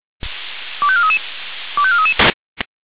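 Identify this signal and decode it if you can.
Five-tone selective call (ZVEI), heard as quick runs of five tones, tones of about 70 ms. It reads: 25433 25433
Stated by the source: ZVEI five-tone selective call 25417, received by SDR at feuerwehr-orlamuende.org (fire brigade Orlamünde).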